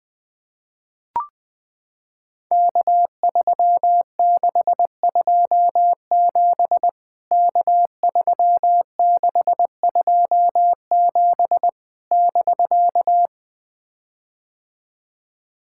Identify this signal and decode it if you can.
Morse code, heard as a keyed tone, about 21 wpm, 700 Hz. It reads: K3627 K3627 <BK>
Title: POTA QSO Practice - 20wpm